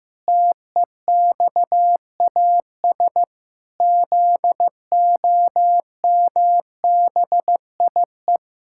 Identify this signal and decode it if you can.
Morse code, heard as a keyed tone, about 15 wpm, 700 Hz. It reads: TEXAS ZOMBIE
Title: Pangram Challenge v2 - 15wpm